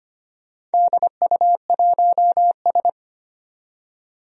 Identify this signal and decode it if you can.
Morse code, heard as a keyed tone, about 25 wpm, 700 Hz. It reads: DU1S
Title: DXCC-100 Call Signs - 4 Characters - 25wpm